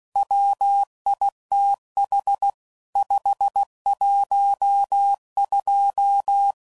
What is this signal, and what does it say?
Morse code, heard as a keyed tone, about 16 wpm, 800 Hz. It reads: WITH 512
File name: CW Code Groups @15WPM / Technical Subjects / AND Gate